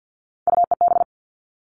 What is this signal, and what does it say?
Morse code, heard as a keyed tone, about 50 wpm, 700 Hz. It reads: UEB
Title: All Cut Numbers - 3-Digit 50wpm